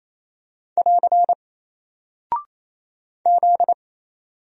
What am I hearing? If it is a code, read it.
Morse code, heard as a keyed tone, about 27 wpm, 700 Hz. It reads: <AR> Z
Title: Single Character (Letter:Number:Punctuation:Prosign) Alt 28wpm